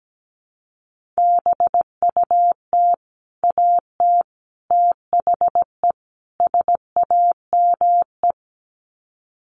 Morse code, heard as a keyed tone, about 17 wpm, 700 Hz. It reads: BUT AT THE SAME
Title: Sets of 4 Words 17wpm